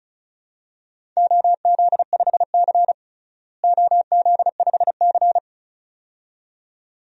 Morse code, heard as a keyed tone, about 36 wpm, 700 Hz. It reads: OZ5C OZ5C
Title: DXCC-100 Call Signs - 4 Characters - Repeated 2x - 35wpm